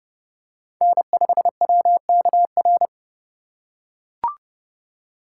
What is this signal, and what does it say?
Morse code, heard as a keyed tone, about 29 wpm, 700 Hz. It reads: N5WKR